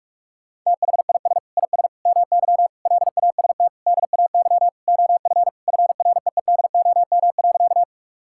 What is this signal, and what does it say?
Morse code, heard as a keyed tone, about 41 wpm, 700 Hz. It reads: THIS IS MY LAST DAY OF FREEDOM.
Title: Sentences from Top 1000 Words - Review Part 5 - 45wpm